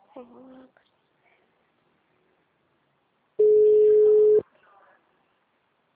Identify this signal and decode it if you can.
Ringback tone (Europe, 425 Hz), with speech.